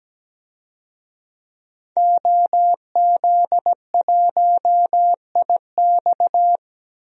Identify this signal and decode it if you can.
Morse code, heard as a keyed tone, about 17 wpm, 700 Hz. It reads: OZ1IX
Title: DXCC-100 Call Signs - 5 Characters - 17wpm